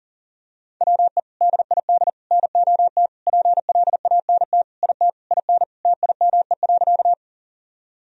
Morse code, heard as a keyed tone, about 40 wpm, 700 Hz. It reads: WE DID NOT PLANT IT IN TIME.
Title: Sentences from Top 1000 Words - Review Part 1 - 40wpm